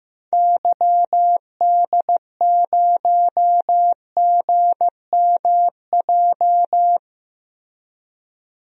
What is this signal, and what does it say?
Morse code, heard as a keyed tone, about 15 wpm, 700 Hz. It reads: YD0GMJ
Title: DXCC-100 Call Signs - 6 Characters - Repeated 2x - 15wpm